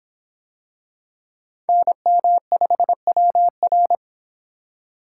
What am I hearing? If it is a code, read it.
Morse code, heard as a keyed tone, about 26 wpm, 700 Hz. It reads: NM5WR